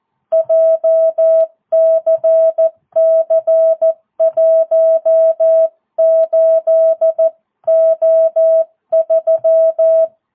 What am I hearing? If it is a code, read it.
Morse code, heard as a keyed tone, about 14 wpm, 650 Hz. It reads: JCC18O3
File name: jh8cbh / 2022cw